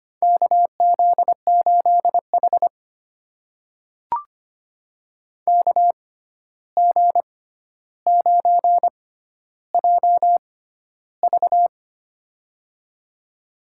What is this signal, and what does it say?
Morse code, heard as a keyed tone, about 25 wpm, spaced as for 12 wpm, 700 Hz. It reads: KZ8H KG9JV
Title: USA Call Signs - 6x Character Spacing - 25wpm